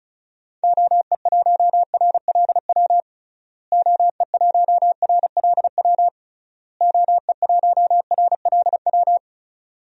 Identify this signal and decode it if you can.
Morse code, heard as a keyed tone, about 36 wpm, 700 Hz. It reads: OE1RLW OE1RLW OE1RLW
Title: DXCC-100 Call Signs - 6 Characters - Repeated 3x - 35wpm